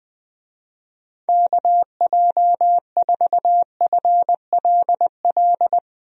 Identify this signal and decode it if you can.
Morse code, heard as a keyed tone, about 20 wpm, 700 Hz. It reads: KJ4FLL
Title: USA Call Signs - 6 Characters - Repeated 3x - 20wpm